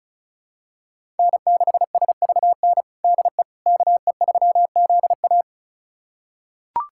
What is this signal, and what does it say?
Morse code, heard as a keyed tone, about 34 wpm, 690 Hz. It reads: N6SVN DE KE3ZA
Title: USA Callsign DE Callsign - 5 Characters - 35wpm